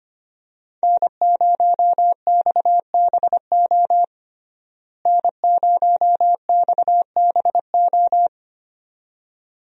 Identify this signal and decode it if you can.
Morse code, heard as a keyed tone, about 25 wpm, 700 Hz. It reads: N0XBO N0XBO
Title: USA Call Signs - 5 Characters - Repeated 2x - 25wpm